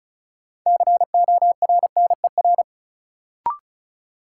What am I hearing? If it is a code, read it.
Morse code, heard as a keyed tone, about 36 wpm, 700 Hz. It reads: CORNER
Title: Sentences from Top 1600 Words - New Word Drill - 35wpm